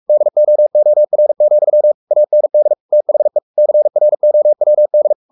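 Morse code, heard as a keyed tone, about 44 wpm, 600 Hz. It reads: DOOR, AND THE CROWD
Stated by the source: Morse code at 43 wpm sending Ken Follett's novel Never in English, from CW med Gnister.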